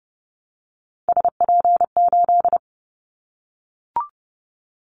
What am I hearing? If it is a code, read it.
Morse code, heard as a keyed tone, about 30 wpm, 700 Hz. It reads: SP8
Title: Callsign Prefixes for Top 25 DXCC - Repeated 3x - 30wpm